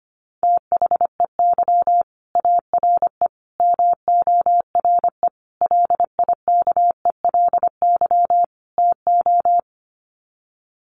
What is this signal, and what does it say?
Morse code, heard as a keyed tone, about 25 wpm, 700 Hz. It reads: THEY ARE MORE LIKELY TO
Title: Sets of 5 Words Encore2 25wpm